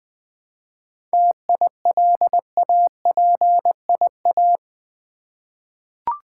Morse code, heard as a keyed tone, about 21 wpm, 700 Hz. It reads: TILAPIA
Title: Difficult Words with Hints - 20wpm